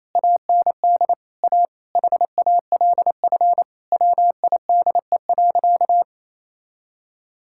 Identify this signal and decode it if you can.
Morse code, heard as a keyed tone, about 28 wpm, 700 Hz. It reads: AND A HALF WIDE.
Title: Sentences from Top 1000 Words - Review Part 6 - 28wpm